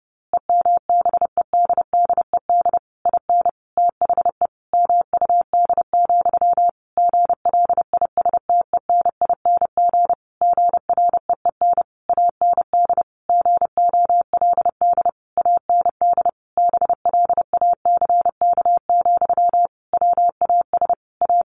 Morse code, heard as a keyed tone, about 30 wpm, 700 Hz. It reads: EMBEDDED IN THE MUD, GLISTENING GREEN AND GOLD AND BLACK, WAS A
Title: A Sound Of Thunder 30wpm